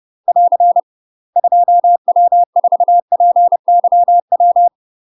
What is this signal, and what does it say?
Morse code, heard as a keyed tone, about 30 wpm, 700 Hz. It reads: <AR> 2W4PYW
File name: UK MASTER QSOs TRAINING / 30WPM / UK MASTER QSOs 0036